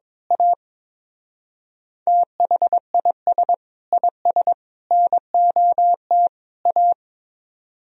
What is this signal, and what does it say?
Morse code, heard as a keyed tone, about 22 wpm, 700 Hz. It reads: A THIS IS NOT A
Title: Speed-Racing - Sets of 4 Words 22wpm